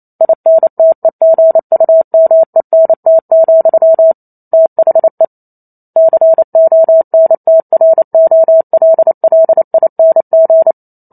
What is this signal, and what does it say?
Morse code, heard as a keyed tone, about 28 wpm, 650 Hz. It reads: INTEGUMENT, THE CONTROLLING